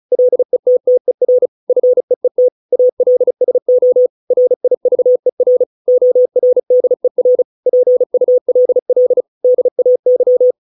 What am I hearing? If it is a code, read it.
Morse code, heard as a keyed tone, about 37 wpm, 500 Hz. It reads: LETTER FEET ALSO RIVER ORDER PULL DAY